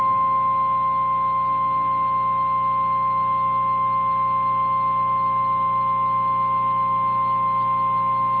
Emergency Alert System (SAME) tones (the 1050 Hz attention tone), with noise. It is heard from 0.5 to 7.7 s.
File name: eas / KNCYKBIE / KNCYKBIE